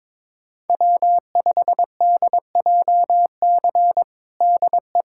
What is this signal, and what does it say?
Morse code, heard as a keyed tone, about 22 wpm, 700 Hz. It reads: W5DJC DE
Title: USA Callsign DE Callsign - 5 Characters - 22wpm